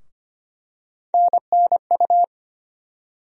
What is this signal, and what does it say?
Morse code, heard as a keyed tone, about 25 wpm, 700 Hz. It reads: NNU
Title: USA Call Sign Postfixes - 3 Letters 25wpm